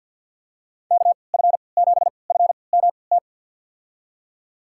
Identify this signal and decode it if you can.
Morse code, heard as a keyed tone, about 48 wpm, spaced as for 28 wpm, 700 Hz. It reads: KF7FMT
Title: USA Call Signs - 6 Characters - 3x Character Spacing - 50wpm